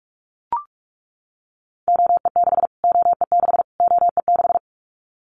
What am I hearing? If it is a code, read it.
Morse code, heard as a keyed tone, about 44 wpm, 700 Hz. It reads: OE6 OE6 OE6
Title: Callsign Prefixes for Top 25 DXCC - Repeated 3x - 45wpm